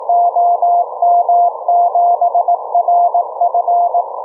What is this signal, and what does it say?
Morse code, heard as a keyed tone, about 18 wpm, 690 Hz. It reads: OM7RF